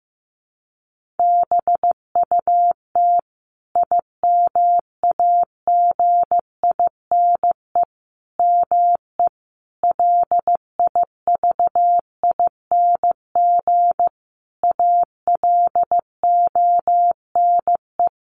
Morse code, heard as a keyed tone, about 15 wpm, 700 Hz. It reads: BUT IMAGINE ME LIVING ALONE
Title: Sentences from Top 1400 Words 15wpm